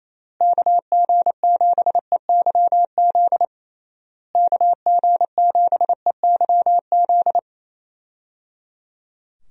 Morse code, heard as a keyed tone, about 27 wpm, 700 Hz. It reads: KG7EYZ KG7EYZ
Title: USA Call Signs - 6 Characters - Repeated 3x - 28wpm